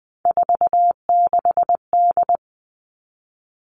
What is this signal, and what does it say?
Morse code, heard as a keyed tone, about 20 wpm, 700 Hz. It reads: 46D
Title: All Cut Numbers - 3-Digit 20wpm